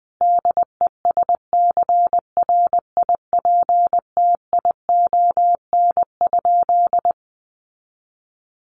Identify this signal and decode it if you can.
Morse code, heard as a keyed tone, about 20 wpm, 700 Hz. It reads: DESCRIPTION?
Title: Sentences from Top 1500 Words 20wpm